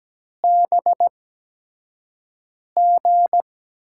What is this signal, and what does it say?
Morse code, heard as a keyed tone, about 17 wpm, spaced as for 5 wpm, 700 Hz. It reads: BG
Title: USA Call Signs - 8x Character Spacing - 17wpm